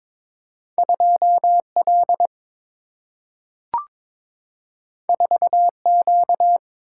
Morse code, heard as a keyed tone, about 22 wpm, 700 Hz. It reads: 2L 4Q
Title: Single-Letter Single-Number 22wpm